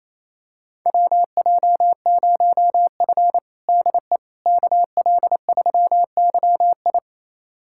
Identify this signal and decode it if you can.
Morse code, heard as a keyed tone, about 28 wpm, 700 Hz. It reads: WJ0F DE KL3YI
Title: USA Callsign DE Callsign 28wpm